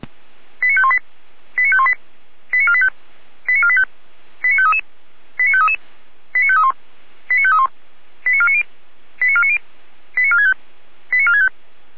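Five-tone selective call (ZVEI), heard as quick runs of five tones, tones of about 70 ms. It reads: 89518 89518 89485 89485 89533 89533 89531 89531 89490 89490 89465 89465
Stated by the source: ZVEI five-tone call received by SDR, from the audio server of Feuerwehr Orlamünde.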